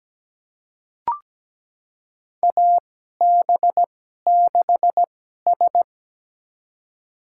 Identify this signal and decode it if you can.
Morse code, heard as a keyed tone, about 17 wpm, spaced as for 12 wpm, 700 Hz. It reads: AB6S
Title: USA Call Signs - 2x Character Spacing - 17wpm